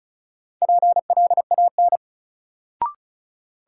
Morse code, heard as a keyed tone, about 36 wpm, 700 Hz. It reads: PLAN
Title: Sentences from Top 500 Words - New Word Drill - 35wpm